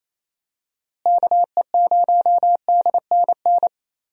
Morse code, heard as a keyed tone, about 27 wpm, 700 Hz. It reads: KE0DNN
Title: USA Call Signs - 6 Characters - Repeated 2x - 28wpm